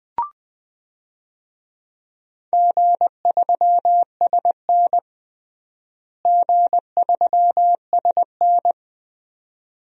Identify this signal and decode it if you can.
Morse code, heard as a keyed tone, about 21 wpm, 700 Hz. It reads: G3SN G3SN
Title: DXCC-100 Call Signs - 4 Characters - Repeated 2x - 20wpm